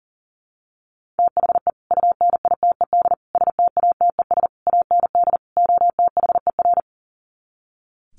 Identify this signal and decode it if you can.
Morse code, heard as a keyed tone, about 38 wpm, 700 Hz. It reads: THE UNITED STATES AND OTHER
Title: Sets of 5 Words Encore 40wpm